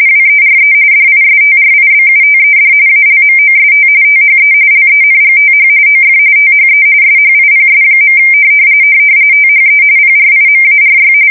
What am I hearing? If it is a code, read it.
RTTY, about 45.45 baud. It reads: WERE DISTRIBUTED ALMOST THE LENGTH OF THE SHIPS. THE PRIMARY
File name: fsk170 / USS-Missouri